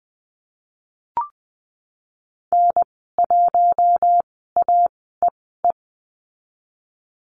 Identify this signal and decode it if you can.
Morse code, heard as a keyed tone, about 19 wpm, spaced as for 14 wpm, 700 Hz. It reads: N1AEE
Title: 055 Lesson 10g - Letter r - Call Signs 20wpm